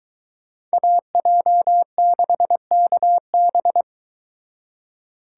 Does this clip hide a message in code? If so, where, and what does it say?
Yes: Morse code from 0.7 to 3.8 s, heard as a keyed tone, about 22 wpm, 700 Hz. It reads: AJ6KB